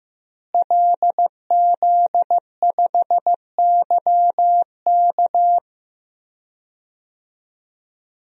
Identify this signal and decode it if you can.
Morse code, heard as a keyed tone, about 15 wpm, 700 Hz. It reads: LZ5YK
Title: DXCC-100 Call Signs - 5 Characters - Repeated 2x - 15wpm